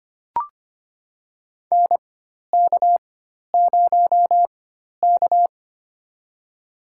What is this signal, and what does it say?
Morse code, heard as a keyed tone, about 25 wpm, spaced as for 12 wpm, 700 Hz. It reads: NK0K
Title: USA Call Signs - 4x Character Spacing - 25wpm